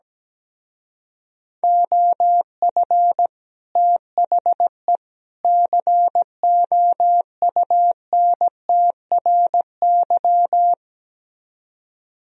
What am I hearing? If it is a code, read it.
Morse code, heard as a keyed tone, about 17 wpm, 700 Hz. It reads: OF THE COUNTRY
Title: Speed-Racing - Sets of 3 Words 17wpm